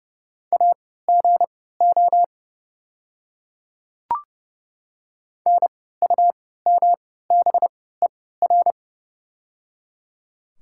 Morse code, heard as a keyed tone, about 31 wpm, spaced as for 17 wpm, 700 Hz. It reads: AGO NUMBER
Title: 3x Character Spacing - Top 500 Words - 30wpm